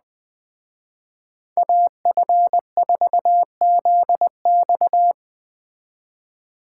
Morse code, heard as a keyed tone, about 20 wpm, 700 Hz. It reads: AF4ZX
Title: Speed Racing - USA 2x2 Call Signs 17wpm